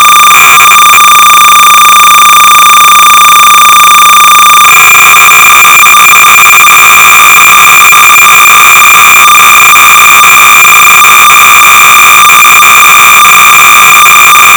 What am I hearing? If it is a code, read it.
Cassette data (Kansas City type).
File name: sorcerer / 300baud / Plot-(1979)(Dick Tolomei)(Quality Software)-BASIC-Sorcerer-300baud